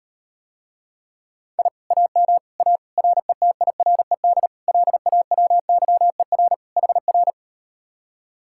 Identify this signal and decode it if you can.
Morse code, heard as a keyed tone, about 39 wpm, 700 Hz. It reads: I AM A RETIRED LAWYER HR